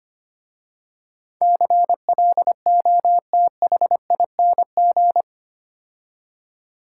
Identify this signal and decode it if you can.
Morse code, heard as a keyed tone, about 25 wpm, 700 Hz. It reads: CLOTHING
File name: 8 Letter Words - First 3 Letters Given - 25wpm